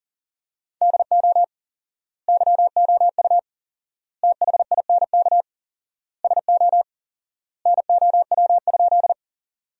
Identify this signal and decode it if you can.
Morse code, heard as a keyed tone, about 40 wpm, 700 Hz. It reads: DO YOU THINK SO NOW?
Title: 4x Word-Spacing - Sentences from Top 200 Words - 40wpm